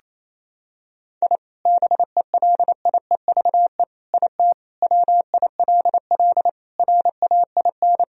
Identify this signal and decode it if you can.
Morse code, heard as a keyed tone, about 28 wpm, 700 Hz. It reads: I BELIEVE IT WILL RAIN
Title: Sentences from Top 1000 Words - Review Part 1 - 28wpm